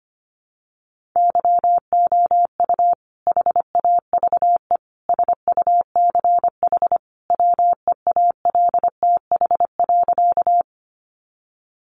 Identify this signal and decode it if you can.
Morse code, heard as a keyed tone, about 25 wpm, 700 Hz. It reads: YOU HAVE SUCH WEALTH.
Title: Sentences from Top 1200 Words 25wpm